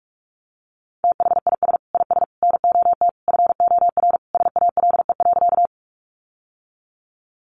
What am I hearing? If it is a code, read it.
Morse code, heard as a keyed tone, about 44 wpm, 700 Hz. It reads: THIS IS NOT FOR SALE.
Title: Sentences from Top 1400 Words 45wpm